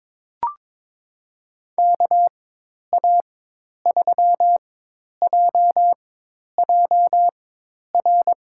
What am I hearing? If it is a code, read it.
Morse code, heard as a keyed tone, about 22 wpm, spaced as for 10 wpm, 700 Hz. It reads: KA3JJR
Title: USA Call Signs - 4x Character Spacing - 22wpm